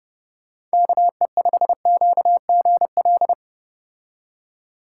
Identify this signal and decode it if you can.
Morse code, heard as a keyed tone, about 34 wpm, 700 Hz. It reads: KE5QGL